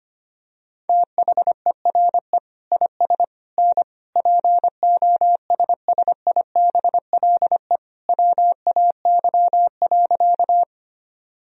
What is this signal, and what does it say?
Morse code, heard as a keyed tone, about 25 wpm, 700 Hz. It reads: THERE IS N POSSIBLE WAY.
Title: Sentences from Top 500 Words and Abbreviations 25wpm